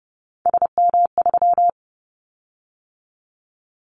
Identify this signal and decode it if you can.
Morse code, heard as a keyed tone, about 30 wpm, 700 Hz. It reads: SM3
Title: Callsign Prefixes for Top 25 DXCC - 30wpm